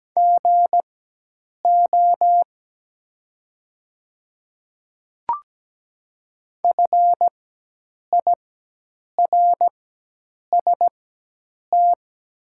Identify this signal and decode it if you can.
Morse code, heard as a keyed tone, about 17 wpm, spaced as for 8 wpm, 700 Hz. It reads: GO FIRST
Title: 4x Character Spacing - Top 100 Words - 17wpm